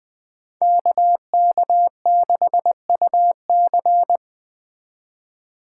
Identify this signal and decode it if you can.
Morse code, heard as a keyed tone, about 20 wpm, 700 Hz. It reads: KK6UC
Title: Speed Racing - USA 2x2 Call Signs 17wpm